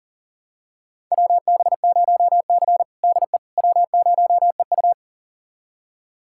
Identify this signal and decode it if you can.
Morse code, heard as a keyed tone, about 40 wpm, 700 Hz. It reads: WB0C DE W0EU
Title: USA Callsign DE Callsign - 4 Characters - 40wpm